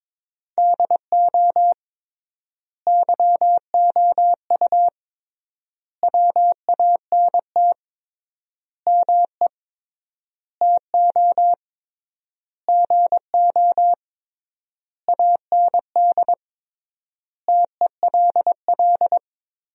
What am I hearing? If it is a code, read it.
Morse code, heard as a keyed tone, about 21 wpm, 700 Hz. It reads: DO YOU WANT ME TO GO AND TELL
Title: 3x Word-Spacing - Sentences from Top 200 Words - 22wpm